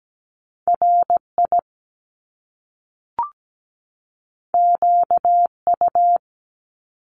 Morse code, heard as a keyed tone, about 17 wpm, 700 Hz. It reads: RI QU
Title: Top Two Letter Combinations in Words 17wpm